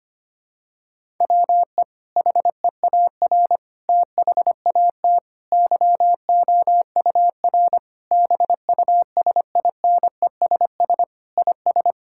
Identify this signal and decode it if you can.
Morse code, heard as a keyed tone, about 25 wpm, 700 Hz. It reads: WE HEAR THAT YOUR BUSINESS IS
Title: Sentences from Top 1000 Words - Review Part 5 - 25wpm